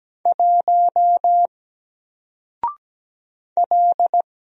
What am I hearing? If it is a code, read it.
Morse code, heard as a keyed tone, about 17 wpm, 700 Hz. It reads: 1 L
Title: Single Letter-Number Alt 17wpm